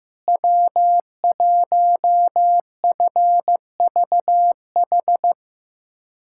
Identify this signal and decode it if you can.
Morse code, heard as a keyed tone, about 16 wpm, 700 Hz. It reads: W1FVH
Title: USA Callsign DE Callsign - 5 Characters - 15wpm